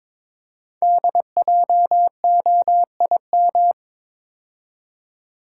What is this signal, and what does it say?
Morse code, heard as a keyed tone, about 22 wpm, 700 Hz. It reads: DJOIM
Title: Code Groups - 5 Letters - 22wpm